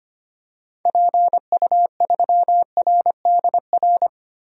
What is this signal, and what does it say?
Morse code, heard as a keyed tone, about 25 wpm, 700 Hz. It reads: PU3RDR
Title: DXCC-100 Call Signs - 6 Characters - 25wpm